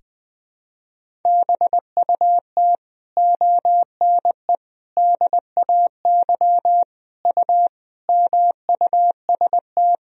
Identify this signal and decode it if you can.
Morse code, heard as a keyed tone, about 21 wpm, 700 Hz. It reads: BUT ONE DAY U MUST
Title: Sentences from Top 400 Words and Abbreviations 20wpm